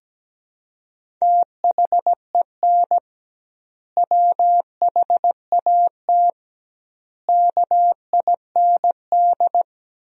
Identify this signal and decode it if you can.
Morse code, heard as a keyed tone, about 17 wpm, 700 Hz. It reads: THEN WHAT KIND
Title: 2x Word-Spacing - Sentences from Top 200 Words - 17wpm